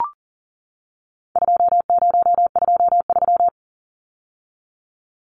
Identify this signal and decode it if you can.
Morse code, heard as a keyed tone, about 40 wpm, 700 Hz. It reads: 2023
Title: Numbers - 4-Digit 40wpm